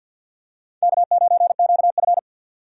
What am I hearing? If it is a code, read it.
Morse code, heard as a keyed tone, about 52 wpm, 700 Hz. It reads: K9QF
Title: USA Call Signs - 4 Characters - 50wpm